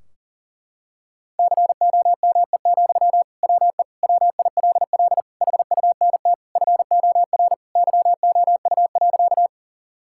Morse code, heard as a keyed tone, about 37 wpm, 700 Hz. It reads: COME, WE WILL HUNT FOR YOU.
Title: Sentences from Top 1600 Words 40wpm